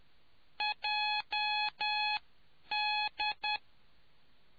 Morse code, heard as a keyed tone, about 10 wpm, spaced as for 8 wpm, 840 Hz. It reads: JD